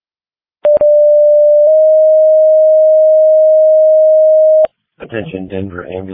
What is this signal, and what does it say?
Two-tone page, heard as a one-second tone, then a three-second tone, about 590 then 620 Hz, with speech. It is heard from 0.6 to 4.6 s.